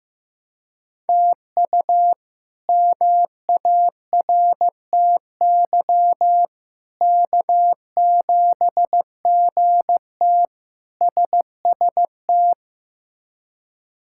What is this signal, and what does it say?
Morse code, heard as a keyed tone, about 15 wpm, 700 Hz. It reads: TU MARTY K7GT SST
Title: K1USN SST QSO Practice - 15wpm